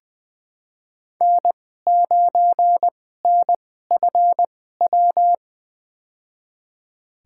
Morse code, heard as a keyed tone, about 20 wpm, spaced as for 15 wpm, 700 Hz. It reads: N9NFW
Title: USA Call Signs - 2x Character Spacing - 20wpm